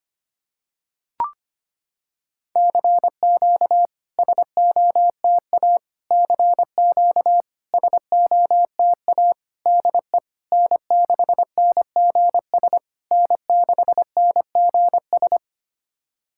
Morse code, heard as a keyed tone, about 25 wpm, 700 Hz. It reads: CQ SOTA CQ SOTA DE N6NGS N6NGS
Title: SOTA QSO Practice 25wpm